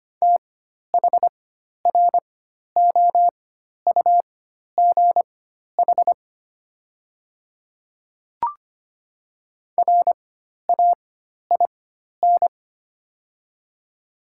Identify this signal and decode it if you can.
Morse code, heard as a keyed tone, about 25 wpm, spaced as for 12 wpm, 700 Hz. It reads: THROUGH RAIN
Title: 4x Character Spacing - Top 300 Words - 25wpm